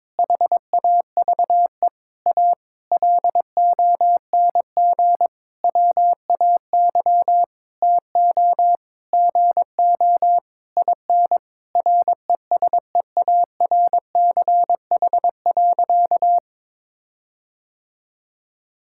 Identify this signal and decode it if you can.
Morse code, heard as a keyed tone, about 22 wpm, 700 Hz. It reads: HAVE A LONG WAY TO GO IN RESEARCH.
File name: Sentences with 9-Word Phrases - 22wpm